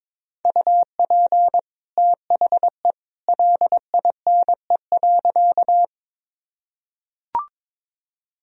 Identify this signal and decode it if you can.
Morse code, heard as a keyed tone, about 22 wpm, 700 Hz. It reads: UP THE LINE.
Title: Sentences from Top 1000 Words - Review Part 1 - 22wpm